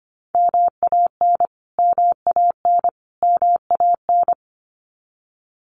Morse code, heard as a keyed tone, about 25 wpm, 700 Hz. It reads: MAN MAN MAN
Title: Top 500 Words Repeated 25wpm